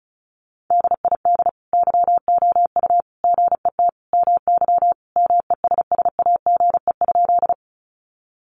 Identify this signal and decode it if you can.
Morse code, heard as a keyed tone, about 34 wpm, 700 Hz. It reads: DID YOU GET MY MESSAGE?
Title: Sentences from Top 1100 Words 35wpm